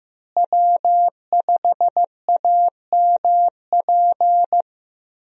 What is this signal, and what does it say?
Morse code, heard as a keyed tone, about 15 wpm, 700 Hz. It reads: W5AMP
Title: USA Call Signs - 5 Characters - Repeated 2x - 15wpm